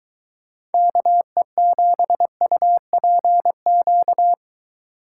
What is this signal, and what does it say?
Morse code, heard as a keyed tone, about 23 wpm, 700 Hz. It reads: KE7UPQ